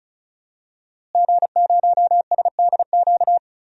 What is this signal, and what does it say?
Morse code, heard as a keyed tone, about 36 wpm, 700 Hz. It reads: G0SDQ
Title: DXCC-100 Call Signs - 5 Characters - Repeated 3x - 35wpm